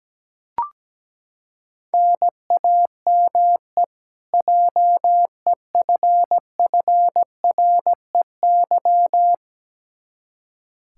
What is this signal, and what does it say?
Morse code, heard as a keyed tone, about 17 wpm, 700 Hz. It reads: NAME JEFFREY
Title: Practice QSO - Exchange 1 - Beginning - 17wpm